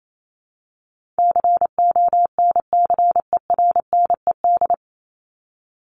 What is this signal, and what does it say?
Morse code, heard as a keyed tone, about 28 wpm, 700 Hz. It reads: CONCERNED
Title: ED Suffix - Top 100 Words with Base Words 28wpm